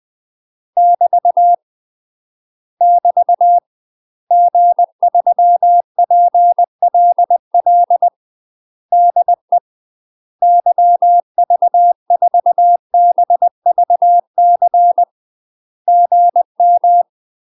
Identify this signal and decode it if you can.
Morse code, heard as a keyed tone, about 20 wpm, 700 Hz. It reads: <BT> <BT> G3PLL DE YV4BVC GM
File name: UK MASTER QSOs TRAINING / 20WPM / UK MASTER QSOs 0024